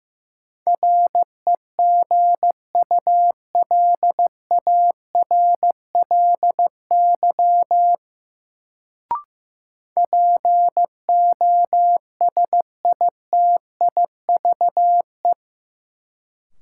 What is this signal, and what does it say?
Morse code, heard as a keyed tone, about 15 wpm, 700 Hz. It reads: REGULARLY POSITIVE
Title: Sentences from Top 1400 Words - New Word Drill Alt - 15wpm